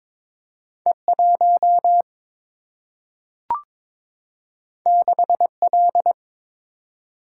Morse code, heard as a keyed tone, about 22 wpm, 700 Hz. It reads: E 1 6 L
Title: Single-Letter Single-Number 22wpm